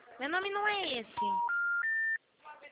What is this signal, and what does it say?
Special information tone, with speech.
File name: out- -2007